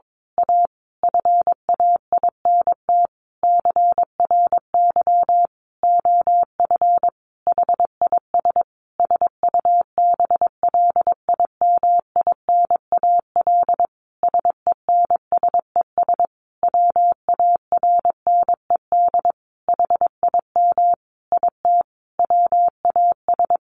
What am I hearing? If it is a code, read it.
Morse code, heard as a keyed tone, about 22 wpm, 700 Hz. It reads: A FAINT CRY OF HIS SUBLIMINAL SENSES WARNED HIM IT WAS
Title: A Sound Of Thunder 22wpm